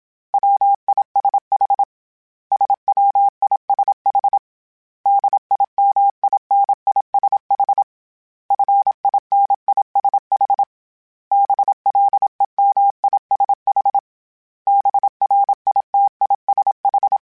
Morse code, heard as a keyed tone, about 27 wpm, 800 Hz. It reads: WISH SWISH DIMINISH FINISH BLEMISH BRITISH